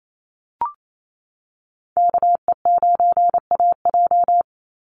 Morse code, heard as a keyed tone, about 28 wpm, 700 Hz. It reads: KE9AJ
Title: NA SOTA Chaser Callsigns 28wpm